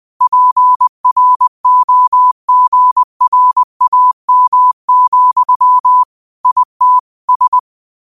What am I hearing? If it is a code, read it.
Morse code, heard as a keyed tone, about 20 wpm, 1000 Hz. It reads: PROGRAM, IT S
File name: qotdfiles / 20wpm